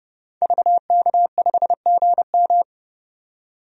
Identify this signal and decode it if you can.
Morse code, heard as a keyed tone, about 31 wpm, 700 Hz. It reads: VK5GM
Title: DXCC-100 Call Signs - 5 Characters - 30wpm